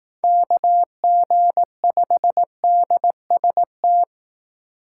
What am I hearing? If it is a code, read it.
Morse code, heard as a keyed tone, about 18 wpm, 700 Hz. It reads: KG5DST
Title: Speed Racing - USA 2x3 Call Signs 15wpm